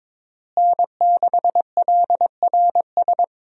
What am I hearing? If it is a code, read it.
Morse code, heard as a keyed tone, about 22 wpm, 700 Hz. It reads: N6LRS
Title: USA Call Signs - 5 Characters - Repeated 3x - 22wpm